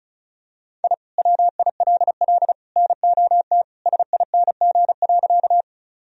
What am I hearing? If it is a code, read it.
Morse code, heard as a keyed tone, about 34 wpm, 700 Hz. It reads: I WILL NOT SING.
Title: Sentences from Top 1000 Words - Review Part 2 - 35wpm